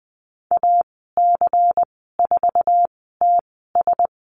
Morse code, heard as a keyed tone, about 20 wpm, spaced as for 14 wpm, 700 Hz. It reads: AC4TS
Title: 193 Lesson 31g - Letter v - Call Signs 20wpm